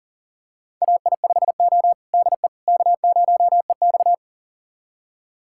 Morse code, heard as a keyed tone, about 40 wpm, 700 Hz. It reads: AI5O DE K0EX
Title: USA Callsign DE Callsign - 4 Characters - 40wpm